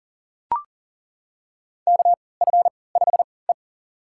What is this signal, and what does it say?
Morse code, heard as a keyed tone, about 40 wpm, spaced as for 23 wpm, 700 Hz. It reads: KF5E
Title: USA Call Signs - 4 Characters - 3x Character Spacing - 40wpm